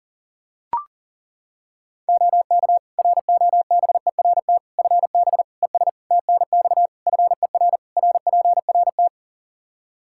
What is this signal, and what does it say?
Morse code, heard as a keyed tone, about 40 wpm, 700 Hz. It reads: OK ROBERT FB ES TNX FER RPRT
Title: K1USN SST QSO Practice 40wpm